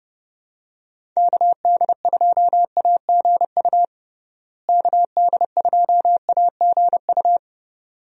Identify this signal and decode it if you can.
Morse code, heard as a keyed tone, about 30 wpm, 700 Hz. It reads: KD2AGU KD2AGU
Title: USA Call Signs - 6 Characters - Repeated 2x - 30wpm